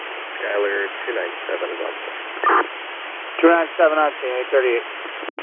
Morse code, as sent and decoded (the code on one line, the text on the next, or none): none